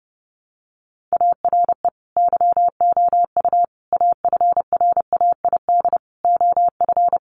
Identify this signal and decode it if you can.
Morse code, heard as a keyed tone, about 30 wpm, 700 Hz. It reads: ARE YOU AFRAID OF
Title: Sentences from Top 1500 Words 30wpm